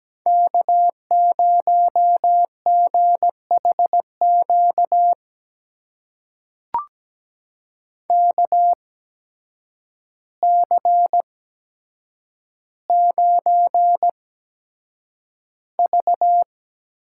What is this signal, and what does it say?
Morse code, heard as a keyed tone, about 17 wpm, spaced as for 8 wpm, 700 Hz. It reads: K0GHQ KC9V
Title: USA Call Signs - 5 Characters - 8x Character Spacing - 17wpm